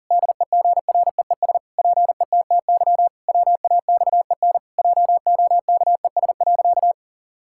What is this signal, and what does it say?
Morse code, heard as a keyed tone, about 40 wpm, 700 Hz. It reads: DEGREES PETTY WAXEN JOKES.
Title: Pangram Challenge v2 - 40wpm